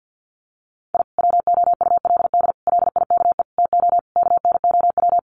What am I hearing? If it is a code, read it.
Morse code, heard as a keyed tone, about 50 wpm, 700 Hz. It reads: I WOULD LIKE TO KNOW
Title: Sets of 5 Words Encore 50wpm